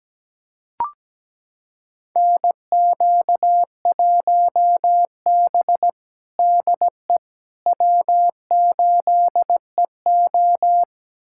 Morse code, heard as a keyed tone, about 17 wpm, 700 Hz. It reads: NQ1B DE W8EO
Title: USA Callsign DE Callsign - 4 Characters - 17wpm